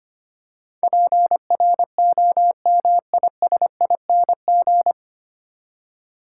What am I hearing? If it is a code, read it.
Morse code, heard as a keyed tone, about 25 wpm, 690 Hz. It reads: PROMISING
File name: practice / 25wpm / 9 Letter Words - First 3 Letters Given - 25wpm